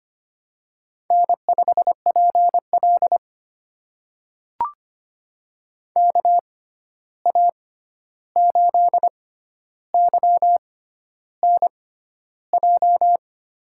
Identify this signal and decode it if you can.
Morse code, heard as a keyed tone, about 25 wpm, spaced as for 11 wpm, 700 Hz. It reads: N5PL KA8YNJ
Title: USA Call Signs - 6x Character Spacing - 25wpm